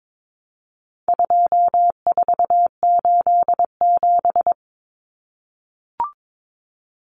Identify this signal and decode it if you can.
Morse code, heard as a keyed tone, about 22 wpm, 700 Hz. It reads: 2487
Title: Numbers - 4-Digit 22wpm